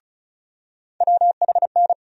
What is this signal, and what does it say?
Morse code, heard as a keyed tone, about 36 wpm, 700 Hz. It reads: WHN
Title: USA Call Sign Postfixes - 3 Letters 35wpm